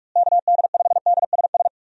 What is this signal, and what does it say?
Morse code, heard as a keyed tone, about 42 wpm, 700 Hz. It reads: KD5DSS
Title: USA Call Signs - 6 Characters - Repeated 2x - 45wpm